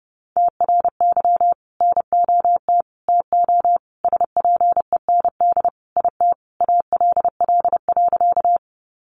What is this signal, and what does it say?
Morse code, heard as a keyed tone, about 30 wpm, 700 Hz. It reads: TRY NOT TO SPEND IT ALL.
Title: Sentences from Top 800 Words 30wpm